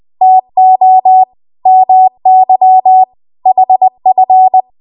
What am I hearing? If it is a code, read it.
Morse code, heard as a keyed tone, about 20 wpm, 750 Hz. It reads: TO MY HF
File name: Archive / 20 WPM / 20WPM